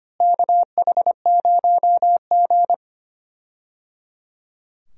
Morse code, heard as a keyed tone, about 25 wpm, 700 Hz. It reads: KH0G
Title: USA Call Signs - 4 Characters - Repeated 3x - 25wpm